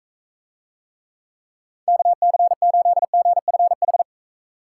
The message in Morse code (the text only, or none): KC8GFH